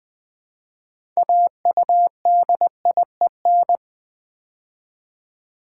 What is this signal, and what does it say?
Morse code, heard as a keyed tone, about 20 wpm, 700 Hz. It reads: AUDIEN
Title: Build A Word - 3 Letters At A Time - 8-Letter Words - 20wpm